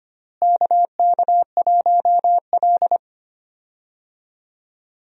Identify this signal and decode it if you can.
Morse code, heard as a keyed tone, about 25 wpm, 700 Hz. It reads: KK1L
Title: USA Call Signs - 4 Characters - Repeated 2x - 25wpm